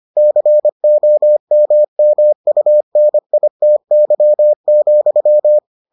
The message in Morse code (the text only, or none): COMMUNITY,